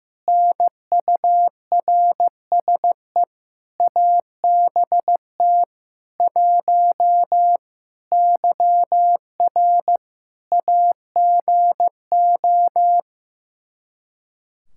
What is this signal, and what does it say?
Morse code, heard as a keyed tone, about 15 wpm, 700 Hz. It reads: NURSE ABT 1 YR AGO
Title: Practice QSO - Exchange 3 - More Info - 15wpm